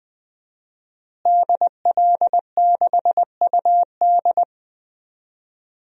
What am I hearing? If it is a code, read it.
Morse code, heard as a keyed tone, about 20 wpm, 700 Hz. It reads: DL6UD
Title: DXCC-100 Call Signs - 5 Characters - Repeated 3x - 20wpm